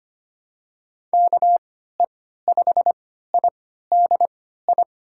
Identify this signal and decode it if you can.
Morse code, heard as a keyed tone, about 25 wpm, spaced as for 14 wpm, 700 Hz. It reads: KE5IDI
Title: USA Call Signs - 6 Characters - 3x Character Spacing - 25wpm